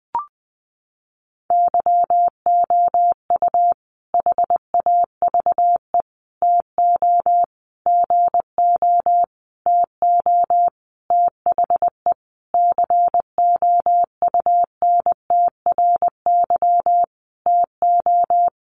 Morse code, heard as a keyed tone, about 20 wpm, 700 Hz. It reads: YOU HAVE TO GO TO THE COUNTRY TO
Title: Sentences from Top 1300 Words 20wpm